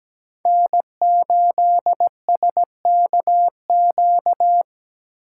Morse code, heard as a keyed tone, about 17 wpm, 700 Hz. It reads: N8SKQ
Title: USA Call Signs - 5 Characters - 4x Character Spacing - 17wpm